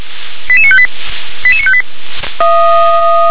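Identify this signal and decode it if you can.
Five-tone selective call (ZVEI), heard as quick runs of five tones, tones of about 70 ms. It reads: 88958 88958
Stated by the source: ZVEI five-tone selective call received by SDR.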